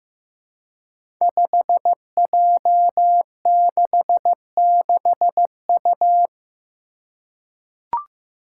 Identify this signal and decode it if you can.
Morse code, heard as a keyed tone, about 15 wpm, 700 Hz. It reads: 5J66U
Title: Five Challenging Letter-Number Combinations 15wpm